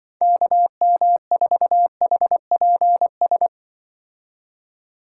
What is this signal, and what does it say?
Morse code, heard as a keyed tone, about 25 wpm, 700 Hz. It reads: KM4HPS
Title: Speed Racing - USA 2x3 Call Signs 20wpm